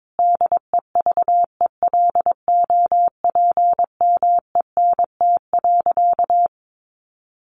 Morse code, heard as a keyed tone, about 22 wpm, 700 Hz. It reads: DEVELOPMENT.
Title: Sentences from Top 1400 Words 22wpm